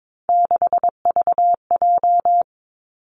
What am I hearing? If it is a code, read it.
Morse code, heard as a keyed tone, about 22 wpm, 700 Hz. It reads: 6VJ
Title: Three Challenging Letter-Number Combinations 22wpm